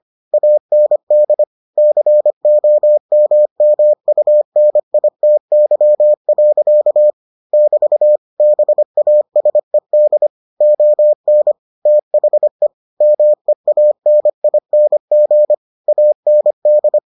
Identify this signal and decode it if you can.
Morse code, heard as a keyed tone, about 25 wpm, 600 Hz. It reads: AND COMMUNITY. <BT> BASED ON THE MEANING AND